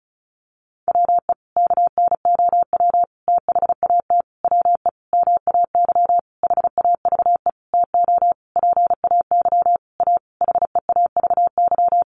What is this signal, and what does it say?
Morse code, heard as a keyed tone, about 34 wpm, 700 Hz. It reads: WE KNOW THAT WE MAY HAVE TO PAY A HEAVY
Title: Sentences from Top 1100 Words 35wpm